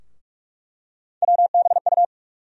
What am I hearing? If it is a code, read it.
Morse code, heard as a keyed tone, about 44 wpm, 700 Hz. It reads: WBU
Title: USA Call Sign Postfixes - 3 Letters 45wpm